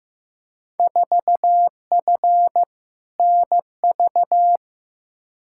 Morse code, heard as a keyed tone, about 15 wpm, 700 Hz. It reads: 4F NV
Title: Field Day Exchanges Encore - 15wpm